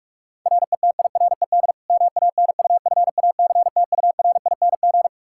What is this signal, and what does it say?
Morse code, heard as a keyed tone, about 46 wpm, 700 Hz. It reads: RETIRED MANUFACTURING
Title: K1USN SST QSO Practice 45wpm